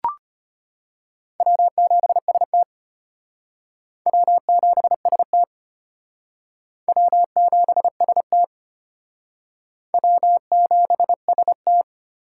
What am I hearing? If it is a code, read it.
Morse code, heard as a keyed tone, about 33 wpm, 700 Hz. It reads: W7ST W7ST W7ST W7ST